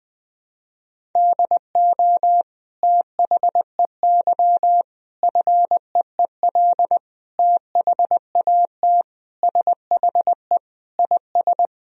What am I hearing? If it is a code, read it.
Morse code, heard as a keyed tone, about 20 wpm, 700 Hz. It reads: DO THEY FEEL THAT SHE IS
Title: Sentences from Top 1000 Words - Review Part 1 - 20wpm